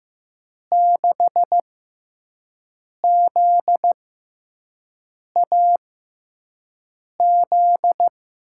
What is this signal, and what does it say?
Morse code, heard as a keyed tone, about 15 wpm, spaced as for 5 wpm, 700 Hz. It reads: 6ZAZ
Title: USA Call Signs - 6 Characters - 6x Character Spacing - 15wpm